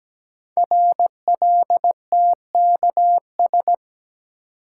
Morse code, heard as a keyed tone, about 17 wpm, 700 Hz. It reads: RLTKS
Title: Code Groups - 5 Letters - 17wpm